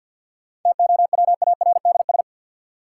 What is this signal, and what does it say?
Morse code, heard as a keyed tone, about 53 wpm, 700 Hz. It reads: TOWARDS
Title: Sentences from Top 1100 Words - New Word Drill Alt - 50wpm